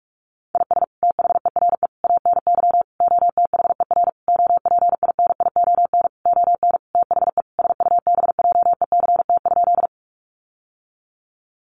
Morse code, heard as a keyed tone, about 44 wpm, 700 Hz. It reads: IS THERE ANY OTHER OPINION ON THE SUBJECT?
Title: Sentences from Top 1200 Words 45wpm